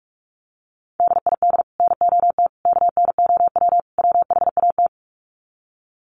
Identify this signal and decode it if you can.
Morse code, heard as a keyed tone, about 44 wpm, 700 Hz. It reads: DID NOT KNOW WHAT
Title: Sets of 4 Words 45wpm